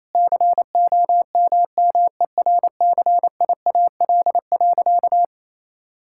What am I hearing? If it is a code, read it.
Morse code, heard as a keyed tone, about 28 wpm, 700 Hz. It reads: COMMERCIAL.
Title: Sentences from Top 1000 Words - Review Part 5 - 28wpm